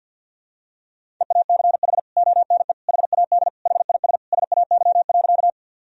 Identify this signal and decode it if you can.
Morse code, heard as a keyed tone, about 47 wpm, 700 Hz. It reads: EACH ONE HAD HIS SAY.
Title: Sentences from Top 1000 Words - Review Part 1 - 50wpm